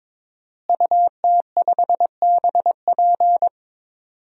Morse code, heard as a keyed tone, about 22 wpm, 700 Hz. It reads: UT5BP
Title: DXCC-100 Call Signs - 5 Characters - Repeated 3x - 22wpm